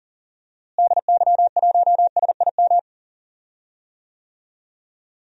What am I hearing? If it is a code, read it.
Morse code, heard as a keyed tone, about 40 wpm, 700 Hz. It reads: DY1SIM
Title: DXCC-100 Call Signs - 6 Characters - Repeated 3x - 40wpm